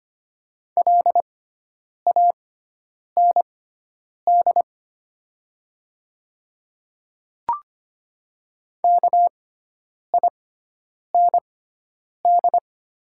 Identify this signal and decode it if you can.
Morse code, heard as a keyed tone, about 25 wpm, spaced as for 9 wpm, 700 Hz. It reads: LAND KIND
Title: 6x Character Spacing - Top 300 Words - 25wpm